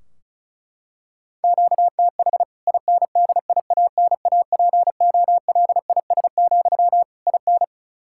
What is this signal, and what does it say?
Morse code, heard as a keyed tone, about 34 wpm, 700 Hz. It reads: QTH INDIANAPOLIS, IN
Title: K1USN SST QSO Practice 35wpm